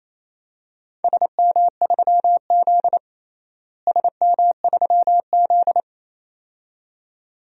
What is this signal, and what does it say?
Morse code, heard as a keyed tone, about 28 wpm, 700 Hz. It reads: SM3Z SM3Z
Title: DXCC-100 Call Signs - 4 Characters - Repeated 2x - 28wpm